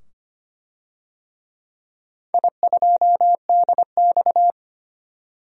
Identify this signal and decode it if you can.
Morse code, heard as a keyed tone, about 25 wpm, 700 Hz. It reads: I2DX
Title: DXCC-100 Call Signs - 4 Characters - 25wpm